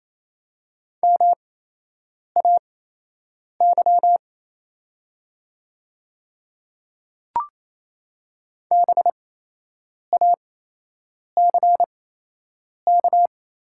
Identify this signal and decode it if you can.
Morse code, heard as a keyed tone, about 28 wpm, spaced as for 8 wpm, 700 Hz. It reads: MAY BACK
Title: 8x Character Spacing - Top 300 Words - 28wpm